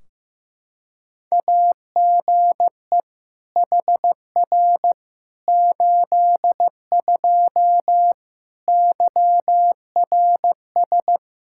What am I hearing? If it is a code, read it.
Morse code, heard as a keyed tone, about 15 wpm, 700 Hz. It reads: AGE HR 82 YRS
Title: K1USN SST QSO Practice 15wpm